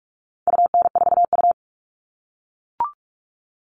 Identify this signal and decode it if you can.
Morse code, heard as a keyed tone, about 44 wpm, 700 Hz. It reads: UN4U
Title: All Cut Numbers - 4-Digit 45wpm